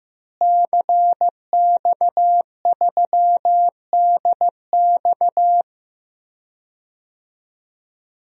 Morse code, heard as a keyed tone, about 15 wpm, 700 Hz. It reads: CX3DX
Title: DXCC-100 Call Signs - 5 Characters - Repeated 3x - 15wpm